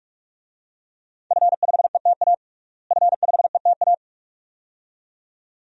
Morse code, heard as a keyed tone, about 46 wpm, 700 Hz. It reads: F5ETA F5ETA
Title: DXCC-100 Call Signs - 5 Characters - Repeated 2x - 45wpm